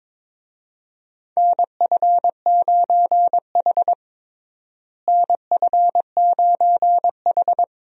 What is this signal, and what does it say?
Morse code, heard as a keyed tone, about 22 wpm, 700 Hz. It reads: NF9H NF9H
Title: USA Call Signs - 4 Characters - Repeated 2x - 22wpm